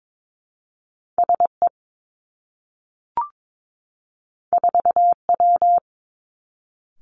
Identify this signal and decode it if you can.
Morse code, heard as a keyed tone, about 22 wpm, 700 Hz. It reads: S E 4 W
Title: Two Challenging Letter-Number Combinations 22wpm